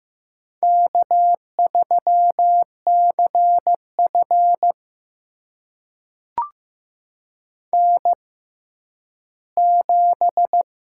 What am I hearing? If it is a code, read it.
Morse code, heard as a keyed tone, about 15 wpm, spaced as for 10 wpm, 700 Hz. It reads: K3CF N7
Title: USA Call Signs - 6x Character Spacing - 15wpm